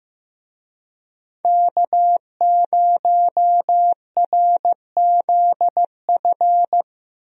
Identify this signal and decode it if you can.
Morse code, heard as a keyed tone, about 15 wpm, 700 Hz. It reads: K0RZF
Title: USA Call Signs - 5 Characters - Repeated 2x - 15wpm